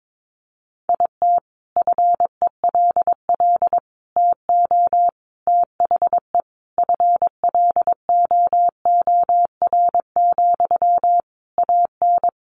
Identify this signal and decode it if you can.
Morse code, heard as a keyed tone, about 22 wpm, 700 Hz. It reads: IT FELL TO THE FLOOR, AN
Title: A Sound Of Thunder 22wpm